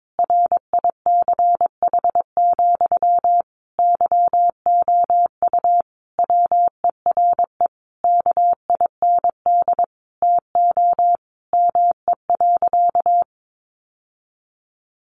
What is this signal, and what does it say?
Morse code, heard as a keyed tone, about 22 wpm, 700 Hz. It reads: RICH, YOU WERE KIND TO ME.
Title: Sentences from Top 1100 Words 22wpm